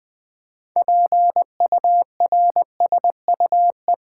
Morse code, heard as a keyed tone, about 20 wpm, 700 Hz. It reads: PURSUE
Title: 6 Letter Words - First 3 Letters Given - 20wpm